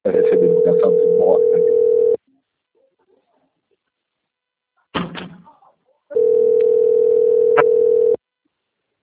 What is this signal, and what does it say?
Ringback tone (North America), 440 + 480 Hz.